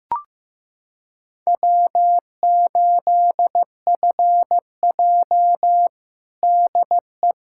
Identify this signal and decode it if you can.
Morse code, heard as a keyed tone, about 14 wpm, 700 Hz. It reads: W8FJ DE
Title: Practice QSO - Exchange 4 - Ending - 15wpm